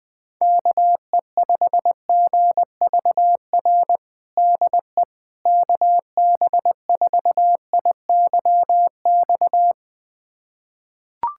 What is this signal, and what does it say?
Morse code, heard as a keyed tone, about 20 wpm, 700 Hz. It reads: KE5GVR DE KB4IYX
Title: USA Callsign DE Callsign 20wpm